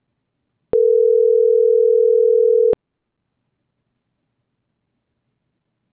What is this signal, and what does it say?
Ringback tone (North America), 440 + 480 Hz.